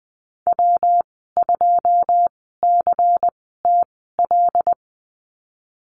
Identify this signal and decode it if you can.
Morse code, heard as a keyed tone, about 19 wpm, spaced as for 14 wpm, 700 Hz. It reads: W2CTL
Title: 122 Lesson 20g - Number 3 - Call Signs 20wpm